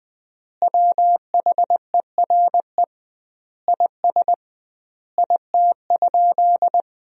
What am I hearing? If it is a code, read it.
Morse code, heard as a keyed tone, about 20 wpm, 700 Hz. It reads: WHERE IS IT?
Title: 2x Word-Spacing - Sentences from Top 200 Words - 20wpm